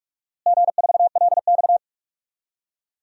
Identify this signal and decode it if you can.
Morse code, heard as a keyed tone, about 46 wpm, 700 Hz. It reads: G4LX
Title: DXCC-100 Call Signs - 4 Characters - Repeated 3x - 45wpm